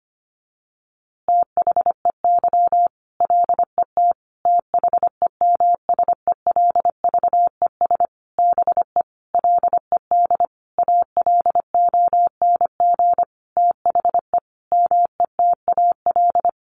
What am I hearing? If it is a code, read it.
Morse code, heard as a keyed tone, about 25 wpm, 700 Hz. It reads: THEY LET THEMSELVES BE LED ALONG THE METAL
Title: A Sound Of Thunder 25wpm